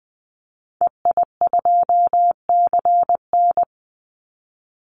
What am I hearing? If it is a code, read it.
Morse code, heard as a keyed tone, about 20 wpm, 700 Hz. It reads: EI2CN
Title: Call Signs Repeated 3x - 20wpm